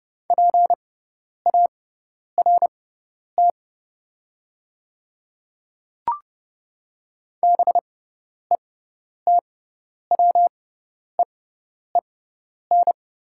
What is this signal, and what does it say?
Morse code, heard as a keyed tone, about 30 wpm, spaced as for 10 wpm, 700 Hz. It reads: PART BETWEEN
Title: 6x Character Spacing - Top 300 Words - 30wpm